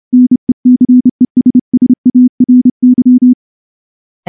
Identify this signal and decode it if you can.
Morse code, heard as a keyed tone, about 28 wpm, 250 Hz. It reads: NECESSARY
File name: morse-training-30WPM